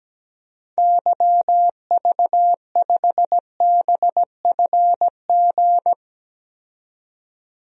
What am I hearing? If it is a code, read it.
Morse code, heard as a keyed tone, about 17 wpm, 700 Hz. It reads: YV5BFG
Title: DXCC-100 Call Signs - 6 Characters - 17wpm